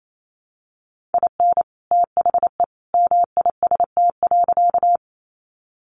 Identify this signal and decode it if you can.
Morse code, heard as a keyed tone, about 28 wpm, 700 Hz. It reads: IN THE MIST.
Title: A Sound Of Thunder 28wpm